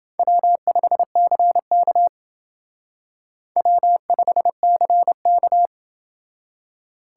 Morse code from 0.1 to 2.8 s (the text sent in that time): W5CK